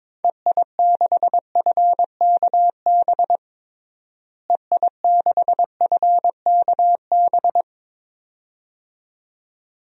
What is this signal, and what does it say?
Morse code, heard as a keyed tone, about 22 wpm, 700 Hz. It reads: EI6FKB EI6FKB
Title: DXCC-100 Call Signs - 6 Characters - Repeated 2x - 22wpm